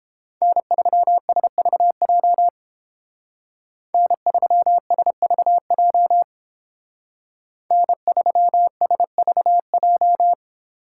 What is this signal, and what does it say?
Morse code, heard as a keyed tone, about 31 wpm, 700 Hz. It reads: N3SVJ N3SVJ N3SVJ